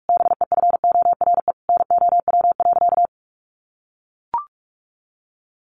Morse code, heard as a keyed tone, about 44 wpm, 700 Hz. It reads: BEFORE NOW.
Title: Sentences from Top 1300 Words 45wpm